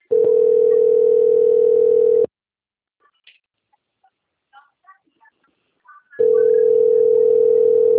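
Ringback tone (North America), 440 + 480 Hz.
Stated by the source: Outgoing phone call recorded at Valinhos - SP.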